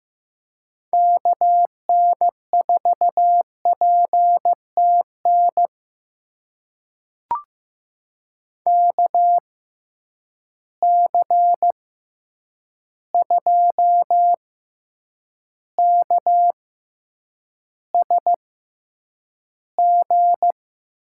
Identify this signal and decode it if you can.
Morse code, heard as a keyed tone, about 15 wpm, spaced as for 8 wpm, 700 Hz. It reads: KN4PTN KC2KSG
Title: USA Call Signs - 6 Characters - 6x Character Spacing - 15wpm